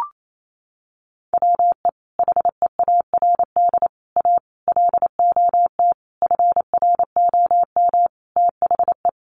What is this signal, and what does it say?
Morse code, heard as a keyed tone, about 28 wpm, 700 Hz. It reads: WE HEARD A LOT FROM THE
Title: Sentences from Top 1100 Words 28wpm